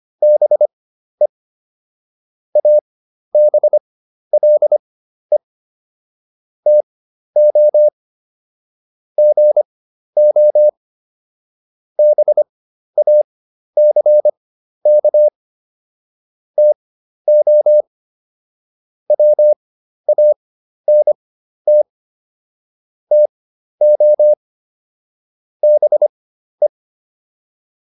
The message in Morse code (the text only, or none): BE ABLE TO GO BACK TO WANT TO BE